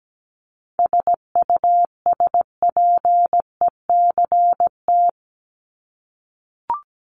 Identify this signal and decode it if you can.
Morse code, heard as a keyed tone, about 17 wpm, 700 Hz. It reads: SUSPECT
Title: Top 1751-2000 Words 17wpm